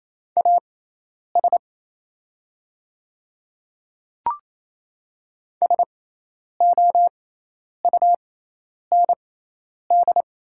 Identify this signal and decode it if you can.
Morse code, heard as a keyed tone, about 27 wpm, spaced as for 10 wpm, 690 Hz. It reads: AS SOUND
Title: 6x Character Spacing - Top 100 Words - 28wpm